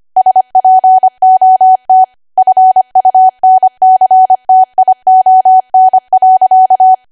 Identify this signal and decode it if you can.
Morse code, heard as a keyed tone, about 25 wpm, 750 Hz. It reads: SPOT FUNCTION.
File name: Archive / 25 WPM / 25WPM